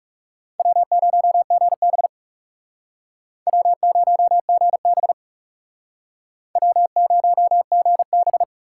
Morse code, heard as a keyed tone, about 39 wpm, 700 Hz. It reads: W0GB W0GB W0GB